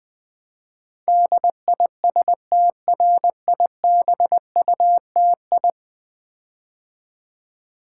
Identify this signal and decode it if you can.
Morse code, heard as a keyed tone, about 20 wpm, 700 Hz. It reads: DISTRIBUTI
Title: Build A Word - 5 Letters At A Time - 12-Letter Words - 20wpm